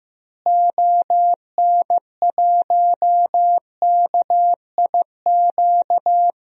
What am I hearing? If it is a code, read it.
Morse code, heard as a keyed tone, about 15 wpm, 700 Hz. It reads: ON1KIQ
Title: DXCC-100 Call Signs - 6 Characters - Repeated 2x - 15wpm